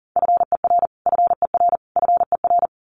Morse code, heard as a keyed tone, about 40 wpm, 700 Hz. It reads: FER FER FER
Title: Abbreviations Repeated 3x - 40wpm